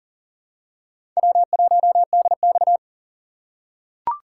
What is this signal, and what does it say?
Morse code, heard as a keyed tone, about 40 wpm, 700 Hz. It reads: W1DX
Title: K1USN SST QSO Practice - 40wpm